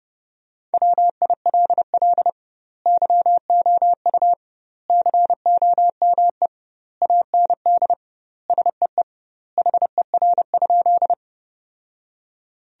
Morse code, heard as a keyed tone, about 30 wpm, 700 Hz. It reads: WILL YOU COME AND SEE HER?
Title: 2x Word-Spacing - Sentences from Top 100 Words - 30wpm